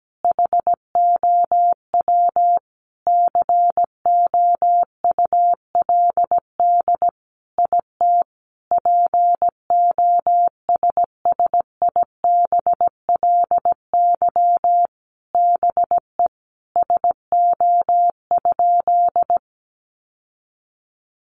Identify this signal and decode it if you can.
Morse code, heard as a keyed tone, about 17 wpm, 700 Hz. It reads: HOW COULD IT POSSIBLY BE SO?
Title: Sentences from Top 1300 Words 17wpm